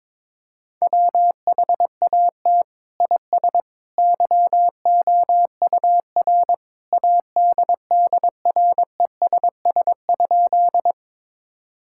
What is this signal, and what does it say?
Morse code, heard as a keyed tone, about 22 wpm, 700 Hz. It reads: WHAT IS YOUR ADDRESS?
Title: Sentences from Top 1000 Words - Review Part 6 - 22wpm